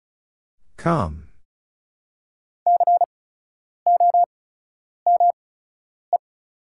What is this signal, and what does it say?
Morse code, heard as a keyed tone, about 36 wpm, spaced as for 10 wpm, 690 Hz. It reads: T COME
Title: 8x Character Spacing - Top 100 Words - 35wpm